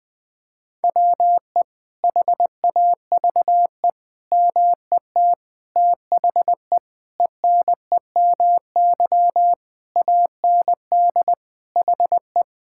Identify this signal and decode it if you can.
Morse code, heard as a keyed tone, about 21 wpm, 700 Hz. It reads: WE HAVE MET THE ENEMY AND HE
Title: Sets of 9 Words - 20wpm